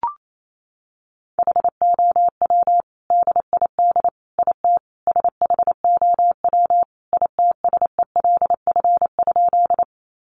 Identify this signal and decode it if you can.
Morse code, heard as a keyed tone, about 28 wpm, 700 Hz. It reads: HOW DID IT SHOW ITSELF?
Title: Sentences from Top 800 Words 28wpm